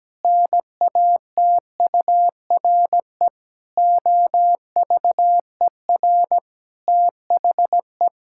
Morse code, heard as a keyed tone, about 17 wpm, 700 Hz. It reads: NATURE OVER THE
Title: Sentences from Top 1000 Words - Review Part 6 - 17wpm